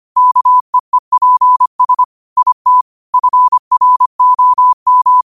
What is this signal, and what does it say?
Morse code, heard as a keyed tone, about 25 wpm, 1000 Hz. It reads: KEEPS IT FROM
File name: qotdfiles / 25wpm